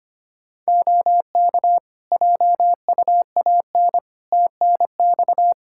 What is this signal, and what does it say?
Morse code, heard as a keyed tone, about 25 wpm, 700 Hz. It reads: OK JUAN TNX
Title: K1USN SST QSO Practice 25wpm